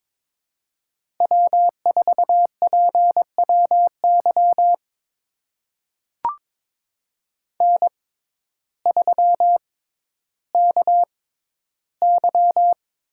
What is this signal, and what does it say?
Morse code, heard as a keyed tone, about 22 wpm, spaced as for 12 wpm, 700 Hz. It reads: W4PWY N3KY
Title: USA Call Signs - 5 Characters - 6x Character Spacing - 22wpm